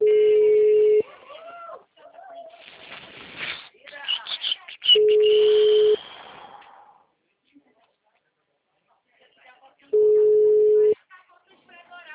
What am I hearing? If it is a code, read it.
Ringback tone (Europe, 425 Hz), with speech.